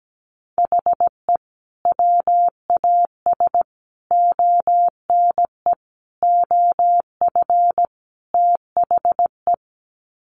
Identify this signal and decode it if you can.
Morse code, heard as a keyed tone, about 17 wpm, 700 Hz. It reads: HE WAS ONE OF THE
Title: Sets of 5 Words 17wpm